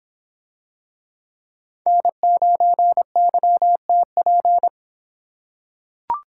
Morse code, heard as a keyed tone, about 26 wpm, 700 Hz. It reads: N9YTP